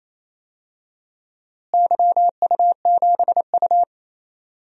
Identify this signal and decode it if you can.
Morse code, heard as a keyed tone, about 28 wpm, 700 Hz. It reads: YU7U
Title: DXCC-100 Call Signs - 4 Characters - Repeated 2x - 28wpm